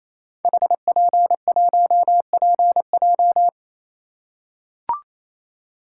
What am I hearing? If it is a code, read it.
Morse code, heard as a keyed tone, about 27 wpm, 700 Hz. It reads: HP1PJ
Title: Code Groups - 5 Letter:Numbers - 28wpm